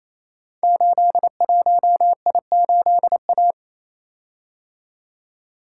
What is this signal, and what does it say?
Morse code, heard as a keyed tone, about 28 wpm, 700 Hz. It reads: 81I8A
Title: Code Groups - 5 Letter:Numbers - 28wpm